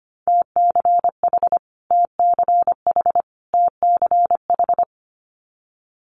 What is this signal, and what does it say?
Morse code, heard as a keyed tone, about 25 wpm, 700 Hz. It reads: TCH TCH TCH
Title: Multi-letter Phonemes Repeated 3x 25wpm